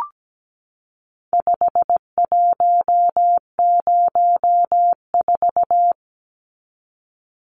Morse code, heard as a keyed tone, about 17 wpm, 700 Hz. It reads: 5104
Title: Numbers - 4-Digit 17wpm